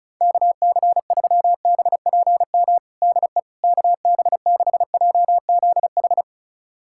Morse code, heard as a keyed tone, about 34 wpm, 700 Hz. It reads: KC3BPM DE KB6JZH
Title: USA Callsign DE Callsign - 6 Characters - 35wpm